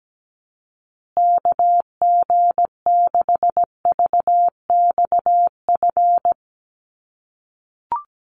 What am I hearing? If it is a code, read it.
Morse code, heard as a keyed tone, about 17 wpm, 700 Hz. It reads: KG6VXF
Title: USA Call Signs Repeated 2x 17wpm